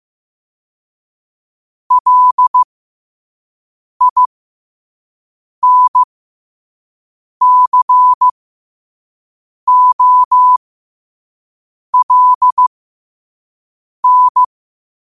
Morse code, heard as a keyed tone, about 15 wpm, spaced as for 5 wpm, 1000 Hz. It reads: LINCOLN
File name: qotdfiles / 5wpm